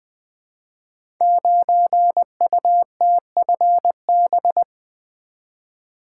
Morse code, heard as a keyed tone, about 20 wpm, 700 Hz. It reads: 9UTFB
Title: Code Groups - 5 Letter:Numbers - 20wpm